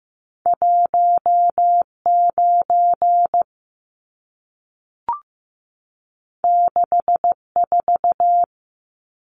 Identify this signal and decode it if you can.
Morse code, heard as a keyed tone, about 15 wpm, 700 Hz. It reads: 19 64
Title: Numbers - 2-Digit 15wpm